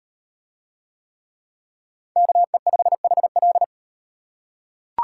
Morse code, heard as a keyed tone, about 35 wpm, 700 Hz. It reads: KE5HL